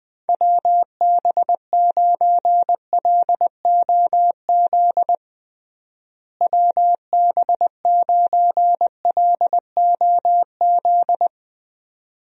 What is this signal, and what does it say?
Morse code, heard as a keyed tone, about 20 wpm, 700 Hz. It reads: WB9LOZ WB9LOZ
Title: USA Call Signs - 6 Characters - Repeated 2x - 20wpm